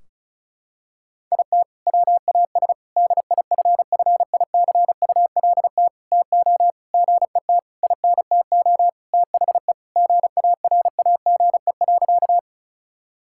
Morse code, heard as a keyed tone, about 36 wpm, 700 Hz. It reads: IT WAS DIFFICULT TO GET INTO THE GARAGE.
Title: Sentences from Top 1600 Words 35wpm